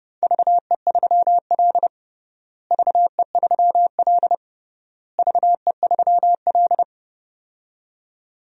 Morse code, heard as a keyed tone, about 30 wpm, 700 Hz. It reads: VE3L VE3L VE3L
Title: DXCC-100 Call Signs - 4 Characters - Repeated 3x - 30wpm